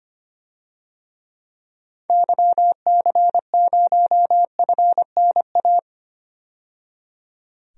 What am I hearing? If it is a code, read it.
Morse code, heard as a keyed tone, about 25 wpm, 690 Hz. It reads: YC0FNA
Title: DXCC-100 Call Signs - 6 Characters - 25wpm